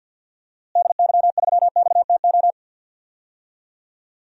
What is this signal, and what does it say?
Morse code, heard as a keyed tone, about 47 wpm, 700 Hz. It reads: DY3XTO
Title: DXCC-100 Call Signs - 6 Characters - Repeated 3x - 50wpm